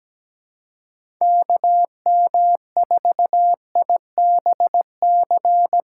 Morse code, heard as a keyed tone, about 17 wpm, 700 Hz. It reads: KM4IBC
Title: USA Call Signs - 6 Characters - 3x Character Spacing - 17wpm